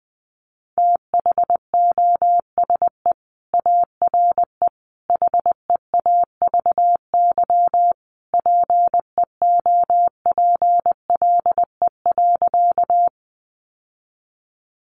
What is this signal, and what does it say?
Morse code, heard as a keyed tone, about 20 wpm, 700 Hz. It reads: THOSE ARE HEAVY PEOPLE.
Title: Sentences from Top 1100 Words 20wpm